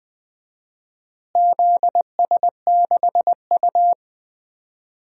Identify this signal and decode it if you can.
Morse code, heard as a keyed tone, about 21 wpm, 700 Hz. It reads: ZS6U
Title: DXCC-100 Call Signs - 4 Characters - Repeated 3x - 20wpm